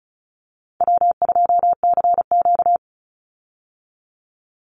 Morse code, heard as a keyed tone, about 34 wpm, 700 Hz. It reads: W2CQ
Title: Call Signs Repeated 3x - 35wpm